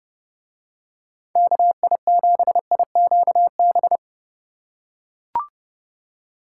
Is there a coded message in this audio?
Yes: Morse code from 1.4 to 4.0 s, heard as a keyed tone, about 30 wpm, 700 Hz. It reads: KI7IQB